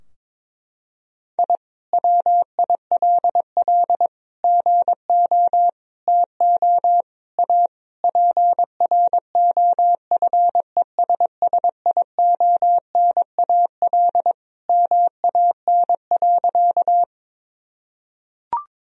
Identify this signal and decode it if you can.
Morse code, heard as a keyed tone, about 22 wpm, 700 Hz. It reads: I WILL GO TO A PROFESSIONAL MAN.
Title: Sentences from Top 1000 Words - Review Part 4 - 22wpm